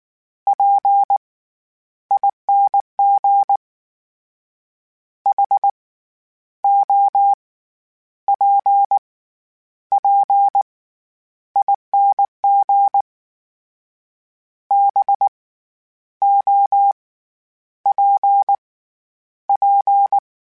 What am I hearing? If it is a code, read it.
Morse code, heard as a keyed tone, about 19 wpm, 800 Hz. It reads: P ING H O P P ING B O P P